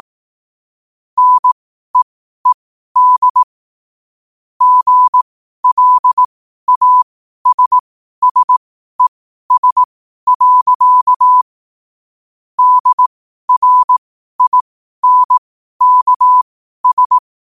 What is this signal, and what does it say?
Morse code, heard as a keyed tone, about 18 wpm, spaced as for 12 wpm, 1000 Hz. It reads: NEED GLASSES. DRINKS
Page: Quotes for Sun, 27 Apr 2025 in Morse Code at 12 words per minute.